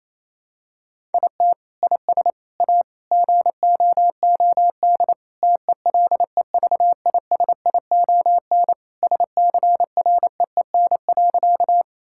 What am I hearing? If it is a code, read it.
Morse code, heard as a keyed tone, about 28 wpm, 700 Hz. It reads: IT IS A GOOD TELEVISION SCREEN.
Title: Sentences from Top 1000 Words - Review Part 6 - 28wpm